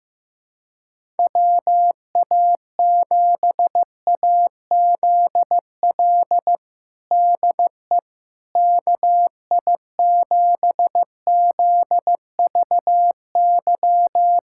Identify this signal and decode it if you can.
Morse code, heard as a keyed tone, about 15 wpm, 700 Hz. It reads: WA7AZL DE KI7ZVY
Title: USA Callsign DE Callsign - 6 Characters - 15wpm